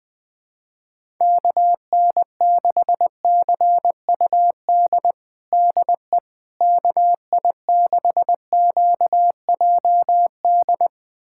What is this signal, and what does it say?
Morse code, heard as a keyed tone, about 20 wpm, 700 Hz. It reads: KN6CUD DE KI6QJD
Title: USA Callsign DE Callsign - 6 Characters - 20wpm